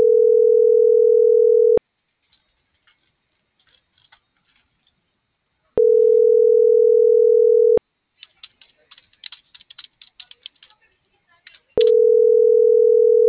Ringback tone (North America), 440 + 480 Hz.